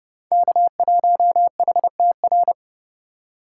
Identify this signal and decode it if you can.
Morse code, heard as a keyed tone, about 30 wpm, 700 Hz. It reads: K1HTR